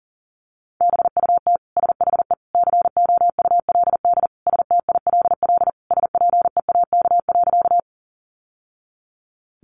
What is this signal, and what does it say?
Morse code, heard as a keyed tone, about 40 wpm, 700 Hz. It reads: BUT SHE COULD STILL SPEAK.
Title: Sentences from Top 1100 Words 40wpm